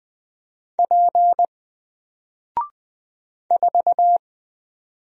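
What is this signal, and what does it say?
Morse code, heard as a keyed tone, about 20 wpm, 700 Hz. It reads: P 4
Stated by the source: Single letter and number Morse drill at 20 wpm, from N4LCD.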